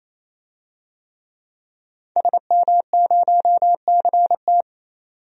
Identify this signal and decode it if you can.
Morse code, heard as a keyed tone, about 28 wpm, 700 Hz. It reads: SM0CT
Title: DXCC-100 Call Signs - 5 Characters - 28wpm